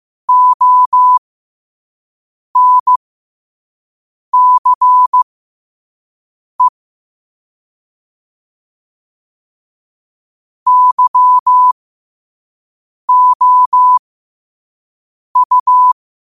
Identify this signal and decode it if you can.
Morse code, heard as a keyed tone, about 15 wpm, spaced as for 5 wpm, 1000 Hz. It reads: ONCE YOU